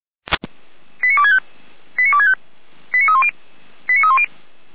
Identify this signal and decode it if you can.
Five-tone selective call (ZVEI), heard as quick runs of five tones, tones of about 70 ms. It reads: 89265 89265 89310 89310